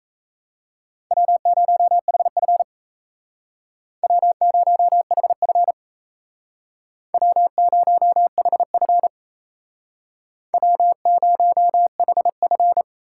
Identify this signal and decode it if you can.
Morse code, heard as a keyed tone, about 37 wpm, 700 Hz. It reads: W0HF W0HF W0HF W0HF